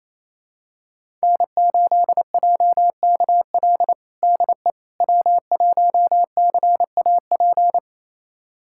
Morse code, heard as a keyed tone, about 28 wpm, 700 Hz. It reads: N8JKL DE W1CAP
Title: USA Callsign DE Callsign - 5 Characters - 28wpm